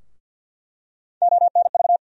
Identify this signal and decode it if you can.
Morse code, heard as a keyed tone, about 48 wpm, 700 Hz. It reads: ONV
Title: USA Call Sign Postfixes - 3 Letters 50wpm